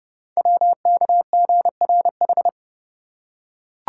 Morse code, heard as a keyed tone, about 30 wpm, 700 Hz. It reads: WKGRH
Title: Code Groups - 5 Letters - 30wpm